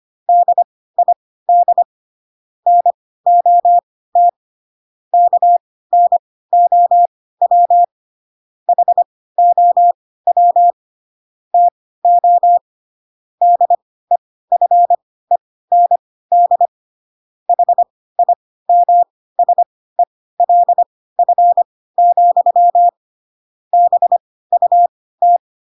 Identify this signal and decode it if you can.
Morse code, heard as a keyed tone, about 25 wpm, spaced as for 16 wpm, 700 Hz. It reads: DID NOT KNOW HOW TO DEFEND HIMSELF, BUT
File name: Pinocchio / Pinocchio ch3 10 25 ik1hge